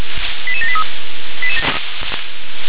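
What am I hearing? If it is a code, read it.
Five-tone selective call (ZVEI), heard as quick runs of five tones, tones of about 70 ms. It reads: 88783
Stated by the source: ZVEI five-tone selective call received on SDR.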